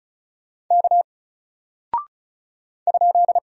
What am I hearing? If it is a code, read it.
Morse code, heard as a keyed tone, about 36 wpm, 700 Hz. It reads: K ?
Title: Single Character (Letter:Number:Punctuation:Prosign) Alt 35wpm